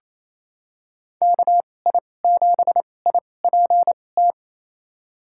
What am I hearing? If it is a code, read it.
Morse code, heard as a keyed tone, about 28 wpm, spaced as for 20 wpm, 700 Hz. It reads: KI7IPT
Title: USA Call Signs - 6 Characters - 2x Character Spacing - 28wpm